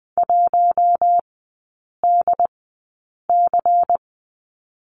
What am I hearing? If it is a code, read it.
Morse code, heard as a keyed tone, about 20 wpm, spaced as for 8 wpm, 700 Hz. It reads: 1DC
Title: 093 Lesson 16e - Letter c - Three Characters 20wpm